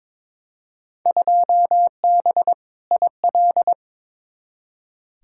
Morse code, heard as a keyed tone, about 22 wpm, 700 Hz. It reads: 2B IL
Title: Field Day Exchanges Encore - 22wpm